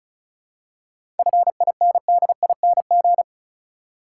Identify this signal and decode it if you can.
Morse code, heard as a keyed tone, about 34 wpm, 700 Hz. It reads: FINDING
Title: Seven Letter Words - 35wpm